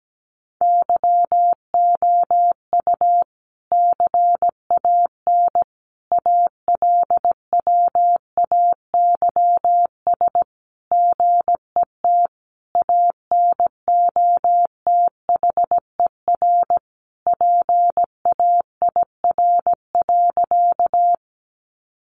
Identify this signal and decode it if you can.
Morse code, heard as a keyed tone, about 17 wpm, 700 Hz. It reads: YOU CAN ALWAYS GET ANOTHER PAIR.
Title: Sentences from Top 1200 Words 17wpm